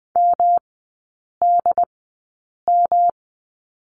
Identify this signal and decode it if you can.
Morse code, heard as a keyed tone, about 20 wpm, spaced as for 8 wpm, 700 Hz. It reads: MDM
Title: 138 Lesson 23e - Letter f - Three Characters 20wpm